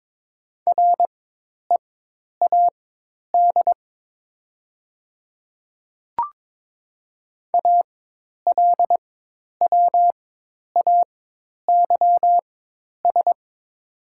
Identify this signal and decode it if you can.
Morse code, heard as a keyed tone, about 22 wpm, spaced as for 10 wpm, 700 Hz. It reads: READ ALWAYS
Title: 4x Character Spacing - Top 300 Words - 22wpm